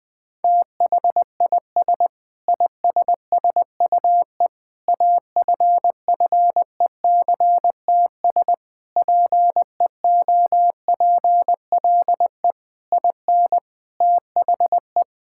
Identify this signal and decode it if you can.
Morse code, heard as a keyed tone, about 19 wpm, 700 Hz. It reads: THIS ISSUE AFFECTS PEOPLE IN THE
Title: Sentences with 8-Word Phrases - Spoken Beginning - 20wpm